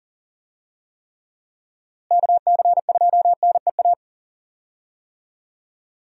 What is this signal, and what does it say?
Morse code, heard as a keyed tone, about 40 wpm, 700 Hz. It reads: KC2NEA
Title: USA Call Signs - 6 Characters - Repeated 2x - 40wpm